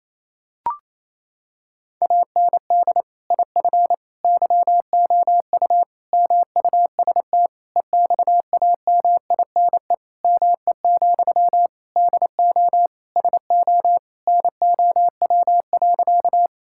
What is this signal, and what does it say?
Morse code, heard as a keyed tone, about 28 wpm, 700 Hz. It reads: AND IF YOU MUST EXAMINE ME, DO SO NOW.
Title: Sentences from Top 1600 Words 28wpm